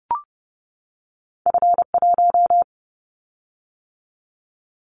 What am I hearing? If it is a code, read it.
Morse code, heard as a keyed tone, about 30 wpm, 700 Hz. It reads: F1
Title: Callsign Prefixes for Top 25 DXCC - 30wpm